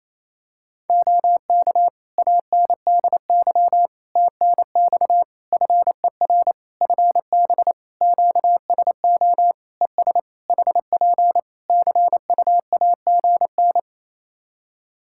Morse code, heard as a keyed tone, about 28 wpm, 700 Hz. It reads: OK ANDY TNX FER FB QSO ES HP CUAGN
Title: K1USN SST QSO Practice 28wpm